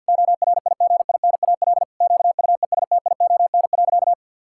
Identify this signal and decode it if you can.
Morse code, heard as a keyed tone, about 50 wpm, 700 Hz. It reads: ORIGINAL QUESTION.
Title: Sentences from Top 1000 Words - Review Part 6 - 50wpm